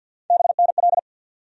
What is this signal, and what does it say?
Morse code, heard as a keyed tone, about 53 wpm, 700 Hz. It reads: BNL
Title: USA Call Sign Postfixes - 3 Letters 50wpm